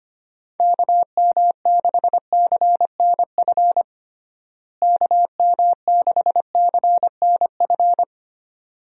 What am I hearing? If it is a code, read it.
Morse code, heard as a keyed tone, about 25 wpm, 700 Hz. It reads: KM6CNF KM6CNF
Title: USA Call Signs - 6 Characters - Repeated 2x - 25wpm